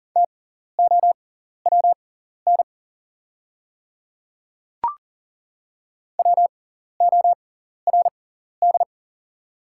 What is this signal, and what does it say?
Morse code, heard as a keyed tone, about 40 wpm, spaced as for 14 wpm, 690 Hz. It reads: TOWN WORD
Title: 6x Character Spacing - Top 500 Words - 40wpm